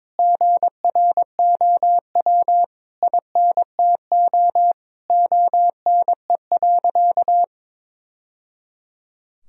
Morse code, heard as a keyed tone, about 22 wpm, 700 Hz. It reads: GROW INTO ONE.
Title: Sentences from Top 1000 Words - Review Part 1 - 22wpm